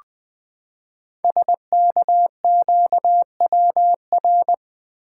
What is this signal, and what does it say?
Morse code, heard as a keyed tone, about 20 wpm, 700 Hz. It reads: SKQWR
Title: Code Groups - 5 Letter:Numbers - 20wpm